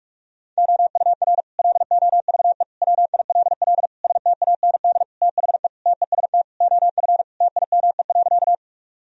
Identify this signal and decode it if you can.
Morse code, heard as a keyed tone, about 46 wpm, 700 Hz. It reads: OUR LOVE WILL STAND THE TEST OF TIME.
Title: Sentences from Top 1000 Words - Review Part 3 - 45wpm